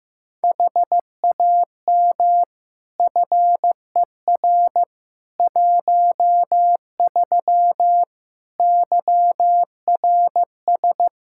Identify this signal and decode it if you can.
Morse code, heard as a keyed tone, about 15 wpm, 700 Hz. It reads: HAM FER 13 YRS
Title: Practice QSO - Exchange 3 - More Info - 15wpm